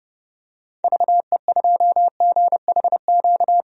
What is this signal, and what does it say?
Morse code, heard as a keyed tone, about 30 wpm, 700 Hz. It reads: VE2GHQ
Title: DXCC-100 Call Signs - 6 Characters - 30wpm